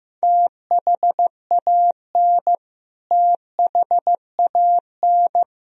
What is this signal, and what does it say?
Morse code, heard as a keyed tone, about 15 wpm, 700 Hz. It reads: THAN THAN
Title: Top 100 Words Repeated 15wpm